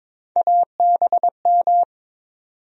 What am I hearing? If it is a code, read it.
Morse code, heard as a keyed tone, about 22 wpm, 700 Hz. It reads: ABM
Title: USA Call Sign Postfixes - 3 Letters 22wpm